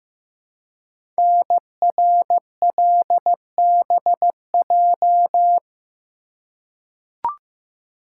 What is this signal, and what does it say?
Morse code, heard as a keyed tone, about 15 wpm, 700 Hz. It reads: NRLBJ
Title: Code Groups - 5 Letters - 15wpm